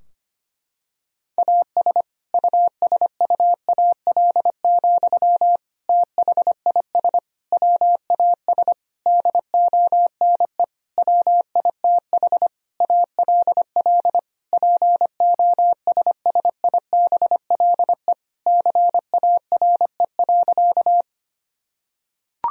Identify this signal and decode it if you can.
Morse code, heard as a keyed tone, about 25 wpm, 700 Hz. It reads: AS USUAL, THIS WAS DONE WITH ALL POSSIBLE CARE.
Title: Sentences from Top 1000 Words - Review Part 2 - 25wpm